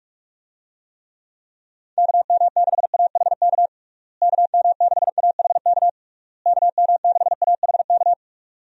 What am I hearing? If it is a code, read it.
Morse code, heard as a keyed tone, about 47 wpm, 700 Hz. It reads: KM6AHK KM6AHK KM6AHK
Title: USA Call Signs - 6 Characters - Repeated 3x - 45wpm